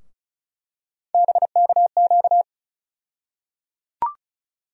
Morse code, heard as a keyed tone, about 34 wpm, 700 Hz. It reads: BKQ
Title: USA Call Sign Postfixes - 3 Letters 35wpm